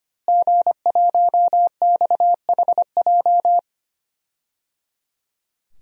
Morse code, heard as a keyed tone, about 25 wpm, 700 Hz. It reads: G1XHJ
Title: DXCC-100 Call Signs - 5 Characters - Repeated 3x - 25wpm